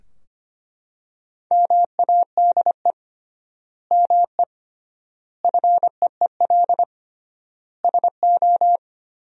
Morse code, heard as a keyed tone, about 25 wpm, 700 Hz. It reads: MADE ME FEEL SO
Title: 3x Word-Spacing - Sentences from Top 400 Words - 25wpm